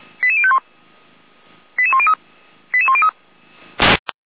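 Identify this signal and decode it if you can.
Five-tone selective call (ZVEI), heard as quick runs of five tones, tones of about 70 ms. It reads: 88051 88103 88103
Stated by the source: ZVEI five-tone selective call received on an SDR.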